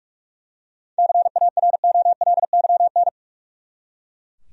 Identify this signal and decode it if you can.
Morse code, heard as a keyed tone, about 41 wpm, 700 Hz. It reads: CAROLYN
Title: USA Common Names - 45wpm